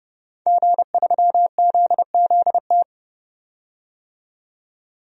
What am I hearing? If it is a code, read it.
Morse code, heard as a keyed tone, about 30 wpm, 700 Hz. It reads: G3ZZT
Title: DXCC-100 Call Signs - 5 Characters - 30wpm